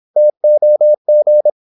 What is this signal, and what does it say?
Morse code, heard as a keyed tone, about 26 wpm, 600 Hz. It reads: TOG
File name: Korte ord DK 26wpm